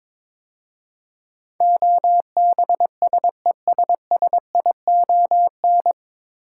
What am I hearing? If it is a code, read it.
Morse code, heard as a keyed tone, about 22 wpm, 700 Hz. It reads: OBSESSION
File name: 9 Letter Words - First 3 Letters Given - 22wpm